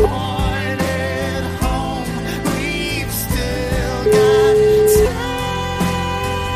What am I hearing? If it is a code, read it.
Ringback tone (Europe, 425 Hz), with noise.